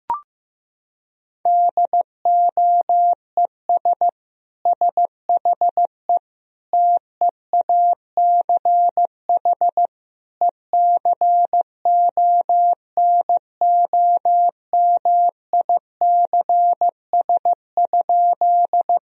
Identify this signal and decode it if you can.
Morse code, heard as a keyed tone, about 14 wpm, 700 Hz. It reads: DOES SHE TEACH ECONOMICS?
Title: Sentences from Top 1000 Words - Review Part 3 - 15wpm